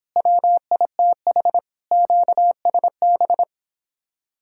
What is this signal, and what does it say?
Morse code, heard as a keyed tone, about 31 wpm, 700 Hz. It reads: WITH QSB